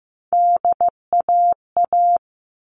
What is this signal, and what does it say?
Morse code, heard as a keyed tone, about 15 wpm, 700 Hz. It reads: DAA
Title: All Cut Numbers - 3-Digit 15wpm